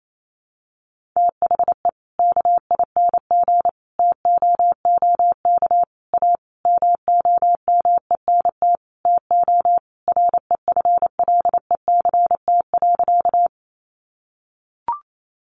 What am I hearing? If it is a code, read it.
Morse code, heard as a keyed tone, about 28 wpm, 700 Hz. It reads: THE KING TOOK A MOMENT TO REFLECT.
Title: Sentences from Top 1500 Words 28wpm